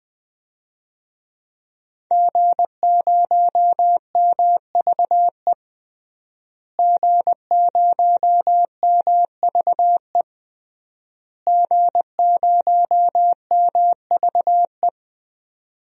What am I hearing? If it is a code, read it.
Morse code, heard as a keyed tone, about 20 wpm, 700 Hz. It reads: G0MVE G0MVE G0MVE
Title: DXCC-100 Call Signs - 5 Characters - Repeated 3x - 20wpm